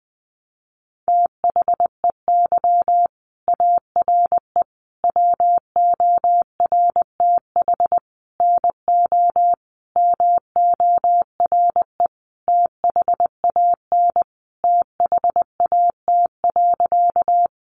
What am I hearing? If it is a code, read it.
Morse code, heard as a keyed tone, about 20 wpm, 700 Hz. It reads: THEY ARE WORTH NO MORE THAN THAT.
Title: Sentences from Top 800 Words 20wpm